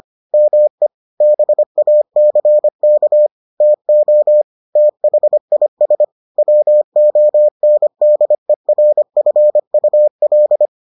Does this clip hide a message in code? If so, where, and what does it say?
Yes: Morse code from 0.3 to 10.7 s, heard as a keyed tone, about 25 wpm, 600 Hz. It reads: ME BACK TO THIS WONDERFUL